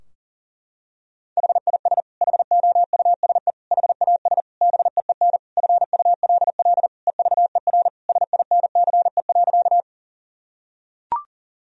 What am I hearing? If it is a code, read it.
Morse code, heard as a keyed tone, about 40 wpm, 700 Hz. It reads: HIS HOUSE HAS BEEN FULL EVER SINCE.
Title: Sentences from Top 1000 Words - Review Part 3 - 40wpm